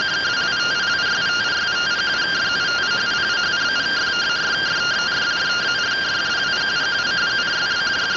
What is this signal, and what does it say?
Two-tone teleprinter signal, about 170 Hz shift.